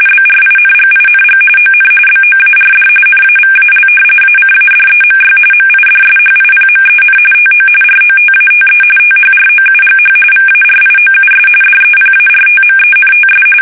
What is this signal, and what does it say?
RTTY, about 45.45 baud. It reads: BRINGING ON BOARD NAVAL ACADEMY MIDSHIPMEN TO ROTATE THROUGH VARIOUS DEPARTMENTS
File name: fsk850 / battleshipnj